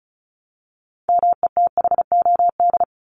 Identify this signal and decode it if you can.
Morse code, heard as a keyed tone, about 34 wpm, 700 Hz. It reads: METHOD
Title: Top 501-750 Words 35wpm